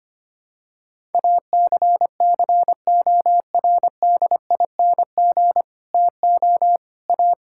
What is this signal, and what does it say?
Morse code, heard as a keyed tone, about 25 wpm, 700 Hz. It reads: ACCORDING TO A
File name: Sets of 9 Words - 25wpm